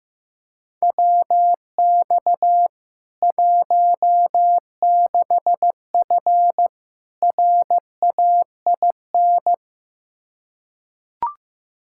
Morse code, heard as a keyed tone, about 15 wpm, 700 Hz. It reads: WX 16F RAIN
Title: K1USN SST QSO Practice 15wpm